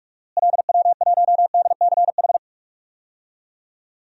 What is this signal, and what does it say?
Morse code, heard as a keyed tone, about 46 wpm, 700 Hz. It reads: LW1DCH
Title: DXCC-100 Call Signs - 6 Characters - Repeated 2x - 45wpm